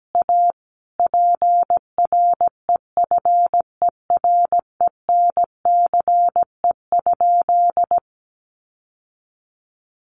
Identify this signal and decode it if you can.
Morse code, heard as a keyed tone, about 17 wpm, 700 Hz. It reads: A PREFERENCE?
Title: Sentences from Top 1500 Words 17wpm